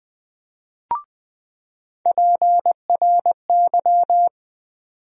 Morse code, heard as a keyed tone, about 20 wpm, 700 Hz. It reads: PRY
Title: USA Call Sign Postfixes - 3 Letters 20wpm